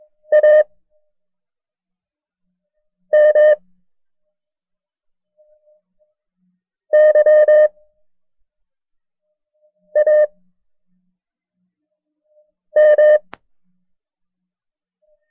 Morse code, heard as a keyed tone, about 22 wpm, 610 Hz. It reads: A M Y A M
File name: FreeFiles / CW-practice / KB6MT-tape1-side2